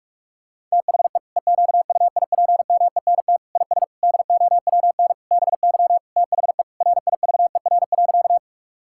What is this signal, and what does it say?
Morse code, heard as a keyed tone, about 46 wpm, 690 Hz. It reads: THE EQUIPMENT IS DOWN BY THE RIVER.
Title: Sentences from Top 1000 Words - Review Part 5 - 45wpm